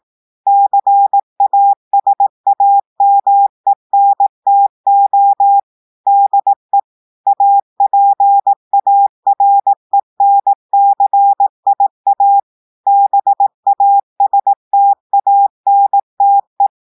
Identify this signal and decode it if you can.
Morse code, heard as a keyed tone, about 18 wpm, 800 Hz. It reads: CASAMENTO DE APARENCIA BASTANTE
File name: audios / portugais / portugais texte en clair 1 vitesse 18 mots minutes